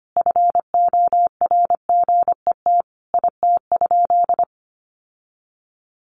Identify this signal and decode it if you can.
Morse code, heard as a keyed tone, about 25 wpm, 700 Hz. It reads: FORGET IT?
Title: Sentences from Top 1200 Words 25wpm